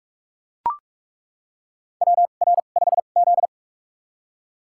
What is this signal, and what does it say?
Morse code, heard as a keyed tone, about 41 wpm, spaced as for 31 wpm, 700 Hz. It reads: WR5Z
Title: USA Call Signs - 2x Character Spacing - 45wpm